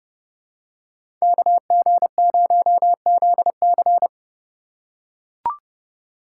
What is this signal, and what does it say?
Morse code, heard as a keyed tone, about 30 wpm, 700 Hz. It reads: KG0ZC
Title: USA Call Signs - 5 Characters - 4x Character Spacing - 30wpm